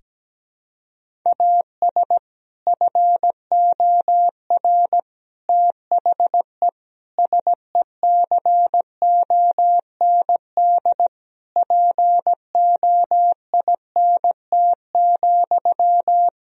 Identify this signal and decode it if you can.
Morse code, heard as a keyed tone, about 17 wpm, 700 Hz. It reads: AS FOR THE SECOND POINT,
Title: Sentences from Top 1000 Words - Review Part 1 - 17wpm